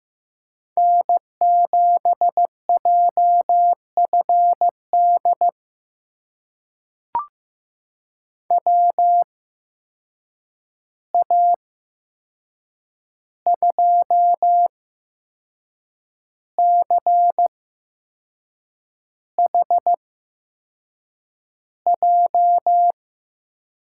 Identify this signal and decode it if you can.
Morse code, heard as a keyed tone, about 15 wpm, spaced as for 6 wpm, 700 Hz. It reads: N7JFD WA2CHJ
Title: USA Call Signs - 8x Character Spacing - 15wpm